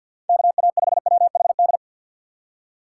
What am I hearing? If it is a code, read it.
Morse code, heard as a keyed tone, about 51 wpm, 700 Hz. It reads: KA5WHD
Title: USA Call Signs - 6 Characters - 4x Character Spacing - 50wpm